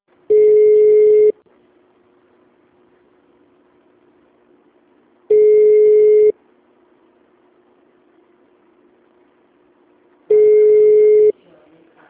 Ringback tone (Europe, 425 Hz).